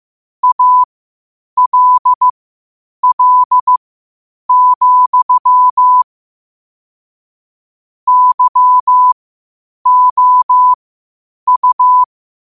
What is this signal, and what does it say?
Morse code, heard as a keyed tone, about 15 wpm, spaced as for 8 wpm, 1000 Hz. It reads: ALL, YOU
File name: qotdfiles / 8wpm